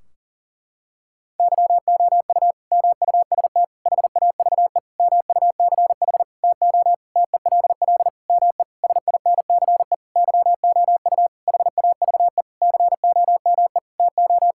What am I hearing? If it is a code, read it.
Morse code, heard as a keyed tone, about 37 wpm, 700 Hz. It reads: YOU MUST HAVE MUCH TO TELL ME SINCE YOU HAVE COME TO
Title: Sentences from Top 1600 Words 40wpm